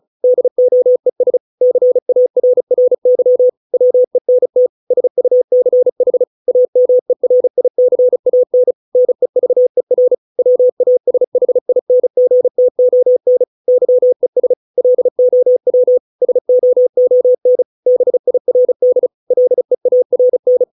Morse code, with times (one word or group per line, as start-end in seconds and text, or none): 0.2-1.4 DOES
1.6-3.5 CARRY
3.7-4.7 WENT
4.9-6.2 SUCH
6.5-8.7 AMERICAN
9.0-10.1 NEVER
10.4-13.4 WASHINGTON
13.7-14.5 YES
14.8-16.0 LOW
16.2-17.6 SOON
17.9-19.1 BIRD
19.3-20.6 LEARN